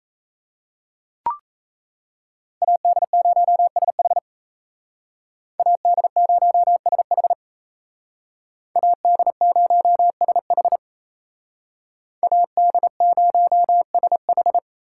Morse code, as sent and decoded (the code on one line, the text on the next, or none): .- -.. ----- ... .... / .- -.. ----- ... .... / .- -.. ----- ... .... / .- -.. ----- ... ....
AD0SH AD0SH AD0SH AD0SH